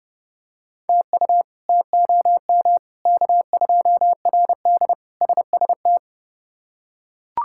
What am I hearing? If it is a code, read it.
Morse code, heard as a keyed tone, about 31 wpm, 700 Hz. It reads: TU TOM K2RD SST
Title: K1USN SST QSO Practice - 30wpm